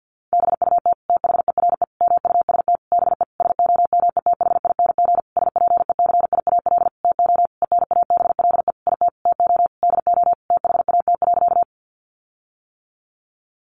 Morse code, heard as a keyed tone, about 50 wpm, 700 Hz. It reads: BUT THERE MUST BE SOMETHING SPECIAL TO ENABLE IT TO DO THAT.
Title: Sentences from Top 1500 Words 50wpm